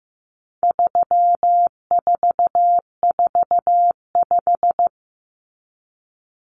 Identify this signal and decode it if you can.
Morse code, heard as a keyed tone, about 15 wpm, 700 Hz. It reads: 3445
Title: Numbers - 4-Digit 15wpm